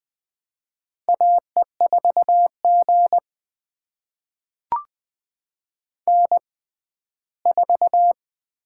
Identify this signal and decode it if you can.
Morse code, heard as a keyed tone, about 20 wpm, spaced as for 14 wpm, 700 Hz. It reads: AE4G N4
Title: USA Call Signs - 4 Characters - 6x Character Spacing - 20wpm